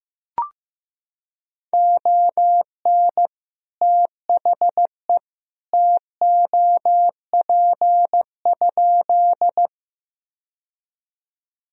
Morse code, heard as a keyed tone, about 15 wpm, 700 Hz. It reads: ON THE TOP?
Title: Sentences from Top 1000 Words - Review Part 2 - 15wpm